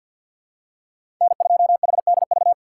Morse code, heard as a keyed tone, about 47 wpm, 700 Hz. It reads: N2HDV
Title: USA Call Signs - 4x Character Spacing - 50wpm